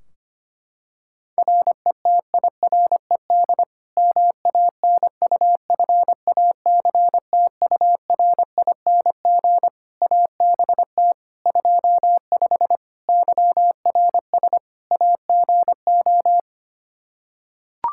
Morse code, heard as a keyed tone, about 25 wpm, 700 Hz. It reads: RETIRED MANUFACTURING ABT 25 YRS AGO
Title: K1USN SST QSO Practice 25wpm